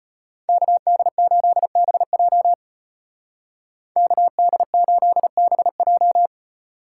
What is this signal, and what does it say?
Morse code, heard as a keyed tone, about 33 wpm, 700 Hz. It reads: KD8BJ KD8BJ